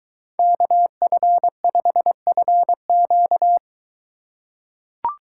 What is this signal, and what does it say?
Morse code, heard as a keyed tone, about 21 wpm, 700 Hz. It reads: KF5FQ